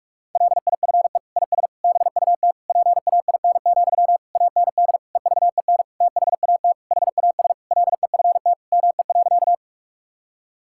Morse code, heard as a keyed tone, about 46 wpm, 700 Hz. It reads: LIFE IS BUT PAIN, AND EVEN THAT HAS LEFT ME.
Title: Sentences from Top 1600 Words 45wpm